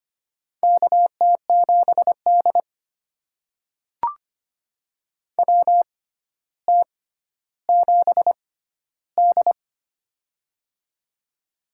Morse code, heard as a keyed tone, about 25 wpm, spaced as for 13 wpm, 700 Hz. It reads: KT7D WT7D
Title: USA Call Signs - 4 Characters - 6x Character Spacing - 25wpm